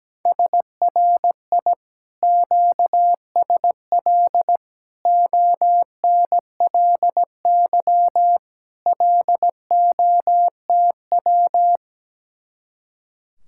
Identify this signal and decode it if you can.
Morse code, heard as a keyed tone, about 17 wpm, 700 Hz. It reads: SRI QSL ONLY LOTW
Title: Speed-Racing Common QSO Phrases Encore - 17wpm